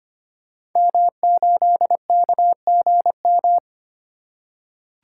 Morse code, heard as a keyed tone, about 25 wpm, 700 Hz. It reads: M8KGM
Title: Code Groups - 5 Letter:Numbers - 25wpm